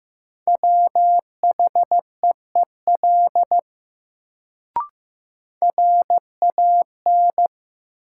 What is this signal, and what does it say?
Morse code, heard as a keyed tone, about 15 wpm, 700 Hz. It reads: WHEEL RAN
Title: Sentences from Top 500 Words - New Word Drill - 15wpm